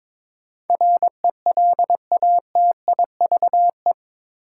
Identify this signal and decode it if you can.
Morse code, heard as a keyed tone, about 21 wpm, 700 Hz. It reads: RELATIVE
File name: practice / 22wpm / Eight Letter Words - 22wpm